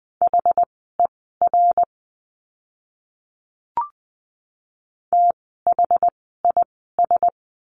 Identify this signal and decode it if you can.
Morse code, heard as a keyed tone, about 20 wpm, spaced as for 15 wpm, 700 Hz. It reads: HER THIS
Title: 237 Lesson 38f - Letter q - Words 20wpm